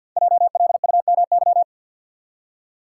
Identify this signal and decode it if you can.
Morse code, heard as a keyed tone, about 51 wpm, 700 Hz. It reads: JLUMY
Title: Code Groups - 5 Letters - 50wpm